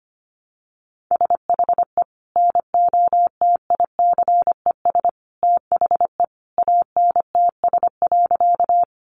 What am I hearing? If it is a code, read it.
Morse code, heard as a keyed tone, about 25 wpm, 700 Hz. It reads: SHE NOTICES THE ANTS.
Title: Additional Easy Phrases 25wpm